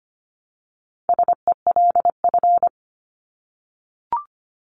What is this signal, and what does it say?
Morse code, heard as a keyed tone, about 25 wpm, 700 Hz. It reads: SELF
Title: Top 500 Words 25wpm